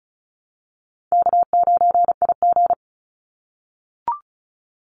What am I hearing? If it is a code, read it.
Morse code, heard as a keyed tone, about 34 wpm, 700 Hz. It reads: K9IG
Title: Call Signs - 35wpm